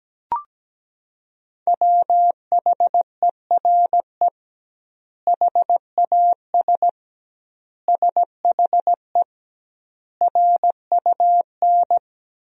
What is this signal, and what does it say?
Morse code, heard as a keyed tone, about 17 wpm, 700 Hz. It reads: WHERE HAS SHE RUN
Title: 2x Word-Spacing - Sentences from Top 300 Words - 17wpm